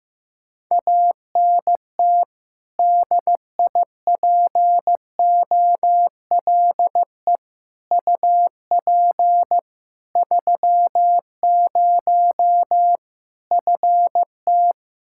Morse code, heard as a keyed tone, about 15 wpm, 700 Hz. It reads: ANT DIPOLE UP 30 FT
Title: K1USN SST QSO Practice 15wpm